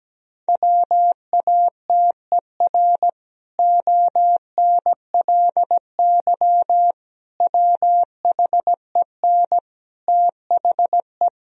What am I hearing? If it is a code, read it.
Morse code, heard as a keyed tone, about 17 wpm, 700 Hz. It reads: WATER ONLY WHEN THE
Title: Sentences from Top 1000 Words - Review Part 4 - 17wpm